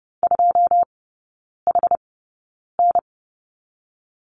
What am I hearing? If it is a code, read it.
Morse code, heard as a keyed tone, about 30 wpm, spaced as for 9 wpm, 700 Hz. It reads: 2HN
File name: Lesson 16e - Letter c - Three Characters 30wpm